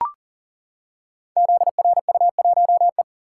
Morse code, heard as a keyed tone, about 41 wpm, 700 Hz. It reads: ZRU1E
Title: Code Groups - 5 Letter:Numbers - 40wpm